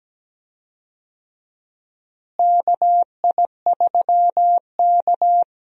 Morse code, heard as a keyed tone, about 17 wpm, 700 Hz. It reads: KI3K
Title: USA Call Signs - 4 Characters - Repeated 3x - 17wpm